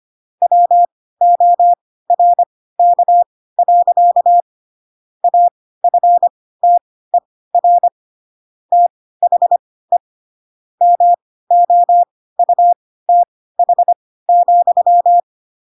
Morse code, heard as a keyed tone, about 25 wpm, spaced as for 16 wpm, 700 Hz. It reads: WORK. AFTER THE MOUTH,
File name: Pinocchio / Pinocchio ch3 10 25 ik1hge